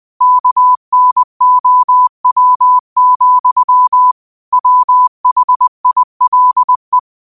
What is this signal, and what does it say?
Morse code, heard as a keyed tone, about 20 wpm, 1000 Hz. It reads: KNOW, WHILE